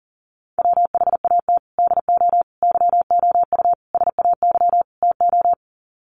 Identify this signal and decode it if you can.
Morse code, heard as a keyed tone, about 40 wpm, 700 Hz. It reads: WHAT DO YOU SAY TO
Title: Sets of 5 Words 40wpm